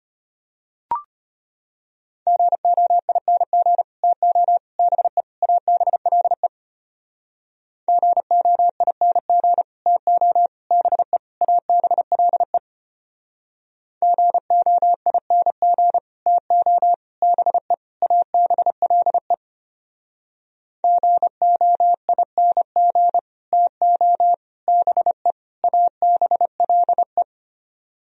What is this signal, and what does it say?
Morse code, heard as a keyed tone, about 32 wpm, 700 Hz. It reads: GOING TO BE ABLE GOING TO BE ABLE GOING TO BE ABLE GOING TO BE ABLE